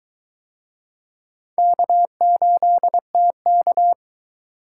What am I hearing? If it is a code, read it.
Morse code, heard as a keyed tone, about 22 wpm, 700 Hz. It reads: K8TK